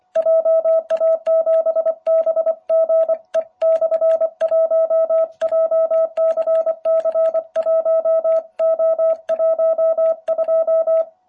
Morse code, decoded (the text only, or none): JA7BGE/1 JCC1O12